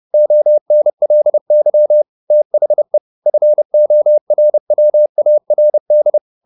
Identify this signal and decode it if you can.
Morse code, heard as a keyed tone, about 30 wpm, 600 Hz. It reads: ONLY THE FORWARD